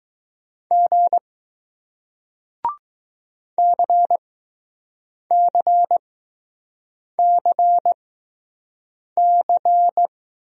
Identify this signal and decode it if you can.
Morse code, heard as a keyed tone, about 20 wpm, spaced as for 6 wpm, 700 Hz. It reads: G CCCC